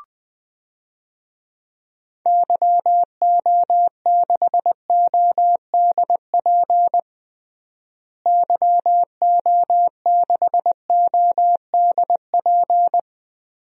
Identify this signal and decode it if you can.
Morse code, heard as a keyed tone, about 20 wpm, 700 Hz. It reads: YO6ODP YO6ODP
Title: DXCC-100 Call Signs - 6 Characters - Repeated 2x - 20wpm